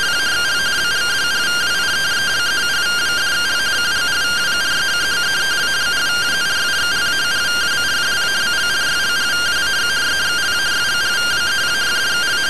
Two-tone teleprinter signal, about 170 Hz shift.
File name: CT SAGITAL 27.gen-12 rtty rx